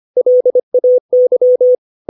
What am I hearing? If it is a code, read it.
Morse code, heard as a keyed tone, about 25 wpm, 500 Hz. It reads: LAY
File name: 632 words / 632 Words 25WPM